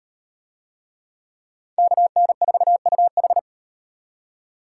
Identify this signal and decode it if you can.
Morse code, heard as a keyed tone, about 36 wpm, 700 Hz. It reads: KN4UH